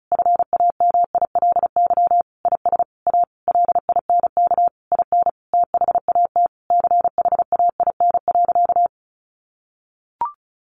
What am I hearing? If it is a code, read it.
Morse code, heard as a keyed tone, about 34 wpm, 700 Hz. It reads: FAMILY IS A LINK IN THAT CHAIN.
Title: Sentences from Top 1500 Words 35wpm